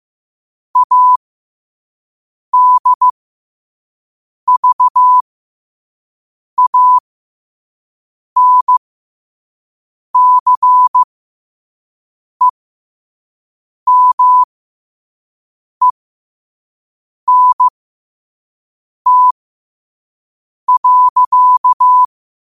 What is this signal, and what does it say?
Morse code, heard as a keyed tone, about 15 wpm, spaced as for 5 wpm, 1000 Hz. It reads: ADVANCEMENT.
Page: New quotes every day in morse code at 5 Words per minute.